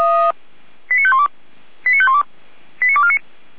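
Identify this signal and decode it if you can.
Five-tone selective call (ZVEI), heard as quick runs of five tones, tones of about 70 ms. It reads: 89512 89512 89249
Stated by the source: Five-tone ZVEI call received by SDR, recorded by Feuerwehr Orlamünde (fire brigade).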